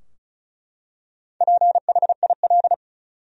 Morse code, heard as a keyed tone, about 34 wpm, 700 Hz. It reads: PHIL
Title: USA Common Names - Max 4 Letters - 35wpm